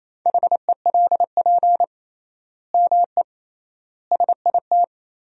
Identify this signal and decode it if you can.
Morse code, heard as a keyed tone, about 27 wpm, 700 Hz. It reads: HELP ME SIT
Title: 3x Word-Spacing - Sentences from Top 500 Words - 28wpm